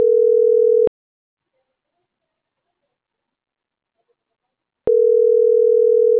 Ringback tone (North America), 440 + 480 Hz.